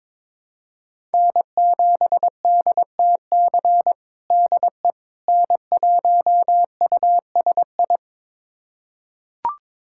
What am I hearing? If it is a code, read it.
Morse code, heard as a keyed tone, about 22 wpm, 700 Hz. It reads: N7DTC DE N1USI
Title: USA Callsign DE Callsign - 5 Characters - 22wpm